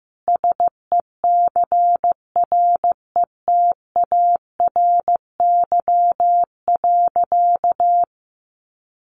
Morse code, heard as a keyed tone, about 15 wpm, 700 Hz. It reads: SECRETARY.
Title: Sentences from Top 1300 Words 15wpm